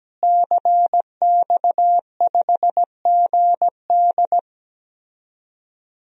Morse code, heard as a keyed tone, about 17 wpm, 700 Hz. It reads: CX5GD
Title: DXCC-100 Call Signs - 5 Characters - Repeated 2x - 17wpm